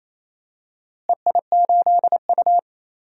Morse code, heard as a keyed tone, about 28 wpm, 700 Hz. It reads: EI8U
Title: DXCC-100 Call Signs - 4 Characters - Repeated 2x - 28wpm